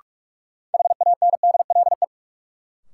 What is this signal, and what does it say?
Morse code, heard as a keyed tone, about 42 wpm, 700 Hz. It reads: HANDLE
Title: Sentences from Top 1100 Words - New Word Drill Alt - 45wpm